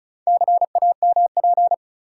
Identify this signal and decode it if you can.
Morse code, heard as a keyed tone, about 36 wpm, 700 Hz. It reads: CAMP
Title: Four Letter Words - 35wpm